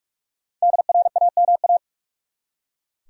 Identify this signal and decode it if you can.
Morse code, heard as a keyed tone, about 47 wpm, 700 Hz. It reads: DRAMA
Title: Five Letter Words - 45wpm